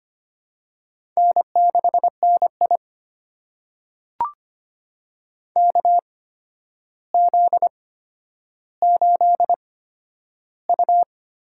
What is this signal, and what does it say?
Morse code, heard as a keyed tone, about 25 wpm, spaced as for 11 wpm, 700 Hz. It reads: N6NI KZ8U
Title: USA Call Signs - 4 Characters - 8x Character Spacing - 25wpm